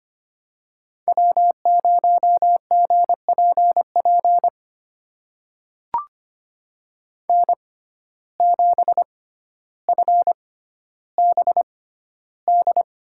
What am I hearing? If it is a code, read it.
Morse code, heard as a keyed tone, about 25 wpm, spaced as for 13 wpm, 700 Hz. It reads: W0GPP N7FBD
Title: USA Call Signs - 5 Characters - 6x Character Spacing - 25wpm